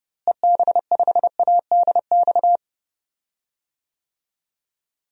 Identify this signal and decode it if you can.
Morse code, heard as a keyed tone, about 30 wpm, 700 Hz. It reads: EB5ADX
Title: DXCC-100 Call Signs - 6 Characters - Repeated 2x - 30wpm